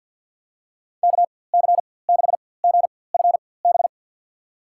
Morse code, heard as a keyed tone, about 47 wpm, spaced as for 23 wpm, 700 Hz. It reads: KC6GFB
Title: USA Call Signs - 6 Characters - 4x Character Spacing - 50wpm